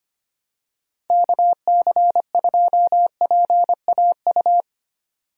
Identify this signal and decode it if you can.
Morse code, heard as a keyed tone, about 25 wpm, 700 Hz. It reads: KC2PAU
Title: USA Call Signs - 6 Characters - 25wpm